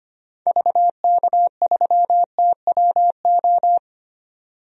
Morse code, heard as a keyed tone, about 25 wpm, 700 Hz. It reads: VK3TWO
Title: DXCC-100 Call Signs - 6 Characters - 25wpm